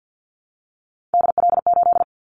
Morse code, heard as a keyed tone, about 50 wpm, 700 Hz. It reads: DL8
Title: Callsign Prefixes for Top 50 DXCC - 50wpm